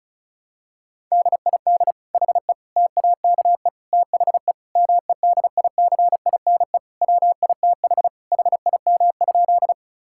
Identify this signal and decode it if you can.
Morse code, heard as a keyed tone, about 34 wpm, 700 Hz. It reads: DID HE TAKE THE MEDICINE WITH HIM?
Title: Sentences from Top 1000 Words - Review Part 6 - 35wpm